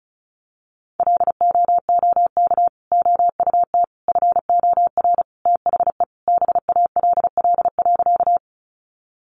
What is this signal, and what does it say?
Morse code, heard as a keyed tone, about 34 wpm, 700 Hz. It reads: LOOK OUT FOR THE BALL.
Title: Sentences from Top 900 Words 35wpm